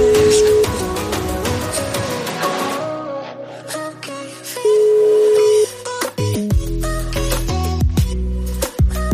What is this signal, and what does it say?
Ringback tone (Europe, 425 Hz), with noise.